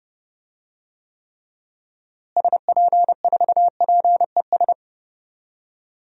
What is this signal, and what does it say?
Morse code, heard as a keyed tone, about 32 wpm, 700 Hz. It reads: SP4PES
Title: DXCC-100 Call Signs - 6 Characters - Repeated 2x - 30wpm